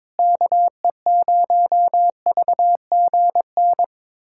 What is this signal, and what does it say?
Morse code, heard as a keyed tone, about 22 wpm, 700 Hz. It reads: KE0VGN
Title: USA Callsign DE Callsign 22wpm